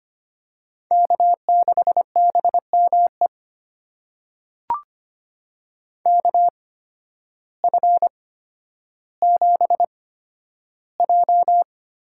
Morse code, heard as a keyed tone, about 25 wpm, spaced as for 12 wpm, 700 Hz. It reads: K6BME KF7J
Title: USA Call Signs - 8x Character Spacing - 25wpm